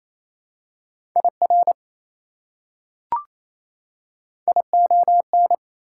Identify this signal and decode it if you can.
Morse code, heard as a keyed tone, about 27 wpm, 700 Hz. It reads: IR ION
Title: Prefixes and Suffixes 28wpm